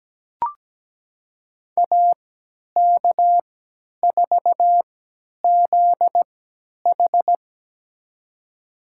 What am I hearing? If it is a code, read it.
Morse code, heard as a keyed tone, about 17 wpm, spaced as for 10 wpm, 700 Hz. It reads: AK4ZH
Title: USA Call Signs - 5 Characters - 3x Character Spacing - 17wpm